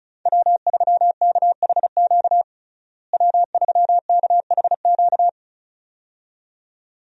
Morse code, heard as a keyed tone, about 34 wpm, 700 Hz. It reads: W3KHQ W3KHQ
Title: USA Call Signs - 5 Characters - Repeated 2x - 35wpm